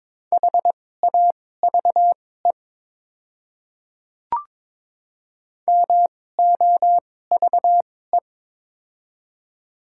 Morse code, heard as a keyed tone, about 22 wpm, spaced as for 16 wpm, 700 Hz. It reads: HAVE MOVE
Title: 2x Character Spacing - Top 200 Words - 22wpm